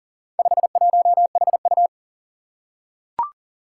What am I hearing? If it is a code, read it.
Morse code, heard as a keyed tone, about 40 wpm, 700 Hz. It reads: 51HU
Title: Four Challenging Letter-Number Combinations 40wpm